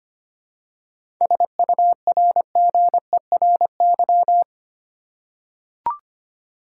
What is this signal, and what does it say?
Morse code, heard as a keyed tone, about 25 wpm, 700 Hz. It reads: SURGERY
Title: Seven Letter Words - 25wpm